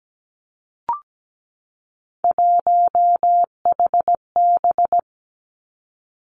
Morse code, heard as a keyed tone, about 17 wpm, 700 Hz. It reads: 1HB
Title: Three Challenging Letter-Number Combinations 17wpm